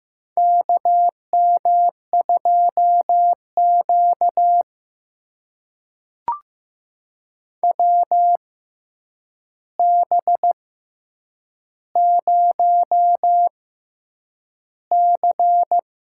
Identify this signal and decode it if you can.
Morse code, heard as a keyed tone, about 15 wpm, spaced as for 8 wpm, 700 Hz. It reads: KM2Q WB0C
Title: USA Call Signs - 4 Characters - 6x Character Spacing - 15wpm